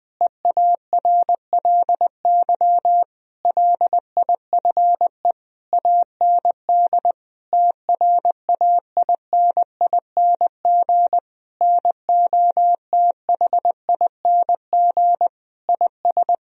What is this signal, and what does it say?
Morse code, heard as a keyed tone, about 20 wpm, 700 Hz. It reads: EARLY LIFE AND TRAINING NOTHING IS
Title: Sentences from Top 1000 Words - Review Part 4 - 20wpm